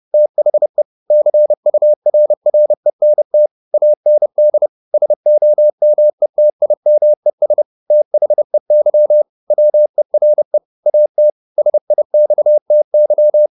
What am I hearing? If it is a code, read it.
Morse code, heard as a keyed tone, about 30 wpm, 600 Hz. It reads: THE CURRENT AND SOMETIMES THEY WERE AT SIXTY
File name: the old man and the sea / the old man and the sea 1 30wpm